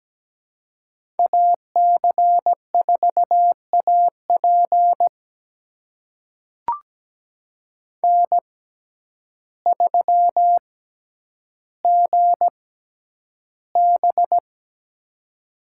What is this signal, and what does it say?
Morse code, heard as a keyed tone, about 17 wpm, spaced as for 9 wpm, 700 Hz. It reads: AC4AP N3GB
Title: USA Call Signs - 6x Character Spacing - 17wpm